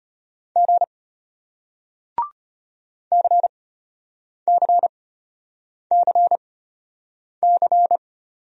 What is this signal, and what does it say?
Morse code, heard as a keyed tone, about 33 wpm, spaced as for 8 wpm, 700 Hz. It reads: G CCCC